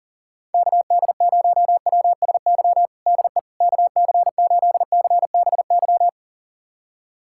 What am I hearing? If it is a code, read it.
Morse code, heard as a keyed tone, about 40 wpm, 700 Hz. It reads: KD0WSY DE KC8CBY
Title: USA Callsign DE Callsign 40wpm